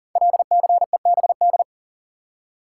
Morse code, heard as a keyed tone, about 41 wpm, 690 Hz. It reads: LCEBD
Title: Code Groups - 5 Letters - 40wpm